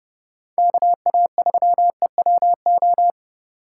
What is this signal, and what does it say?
Morse code, heard as a keyed tone, about 38 wpm, 700 Hz. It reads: KA3EWO